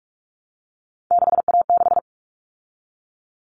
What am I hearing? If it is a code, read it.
Morse code, heard as a keyed tone, about 44 wpm, 700 Hz. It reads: 6A6
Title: All Cut Numbers - 3-Digit 45wpm